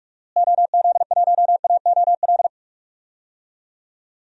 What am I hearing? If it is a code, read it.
Morse code, heard as a keyed tone, about 46 wpm, 700 Hz. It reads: OZ1AOL
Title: DXCC-100 Call Signs - 6 Characters - Repeated 2x - 45wpm